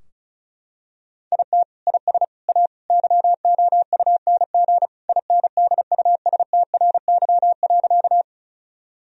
Morse code, heard as a keyed tone, about 36 wpm, 700 Hz. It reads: IT IS A YOUNG INDUSTRY.
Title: Sentences from Top 1000 Words - Review Part 4 - 35wpm